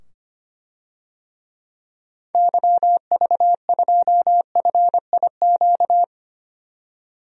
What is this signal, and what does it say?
Morse code, heard as a keyed tone, about 25 wpm, 700 Hz. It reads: YV2FIQ
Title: DXCC-100 Call Signs - 6 Characters - 25wpm